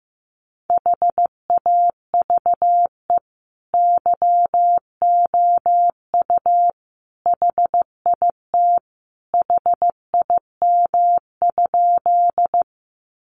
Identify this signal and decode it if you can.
Morse code, heard as a keyed tone, about 15 wpm, 700 Hz. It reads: HAVE YOU HIT HIM?
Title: Sentences from Top 1200 Words 15wpm